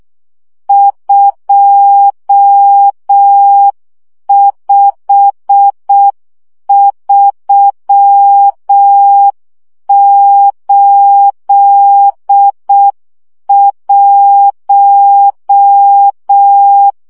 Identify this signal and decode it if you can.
Morse code, heard as a keyed tone, about 6 wpm, 800 Hz. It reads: 25381
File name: morze vizsga 2011-1 5